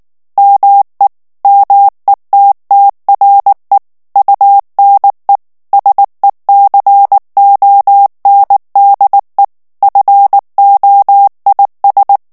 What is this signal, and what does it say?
Morse code, heard as a keyed tone, about 19 wpm, 800 Hz. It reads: ME METTRE UNE SECONDE FOIS
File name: FRANCAIS TEXTE CLAIR 2 VITESSE 18 MOTS MINUTES